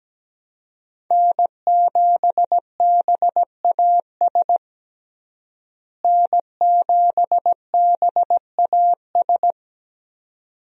Morse code, heard as a keyed tone, about 17 wpm, 700 Hz. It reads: N7BAS N7BAS
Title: USA Call Signs - 5 Characters - Repeated 2x - 17wpm